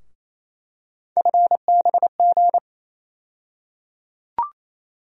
Morse code, heard as a keyed tone, about 28 wpm, 700 Hz. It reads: FBG
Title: USA Call Sign Postfixes - 3 Letters 28wpm